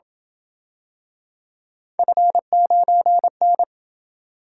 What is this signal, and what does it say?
Morse code, heard as a keyed tone, about 27 wpm, 700 Hz. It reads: F9N